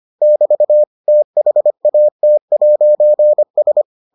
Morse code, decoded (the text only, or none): <BT> THAT'S